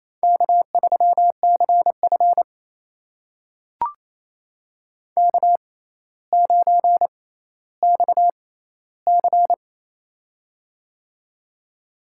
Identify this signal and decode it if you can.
Morse code, heard as a keyed tone, about 28 wpm, spaced as for 14 wpm, 700 Hz. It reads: K3CF K9XC
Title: USA Call Signs - 4 Characters - 6x Character Spacing - 28wpm